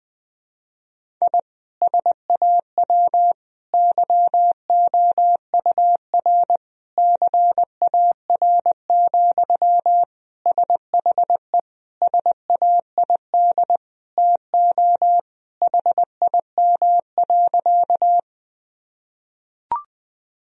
Morse code, heard as a keyed tone, about 21 wpm, 700 Hz. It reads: I SAW YOUR CAR, SHE SAID TO HIM.
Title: Sentences from Top 1000 Words - Review Part 1 - 20wpm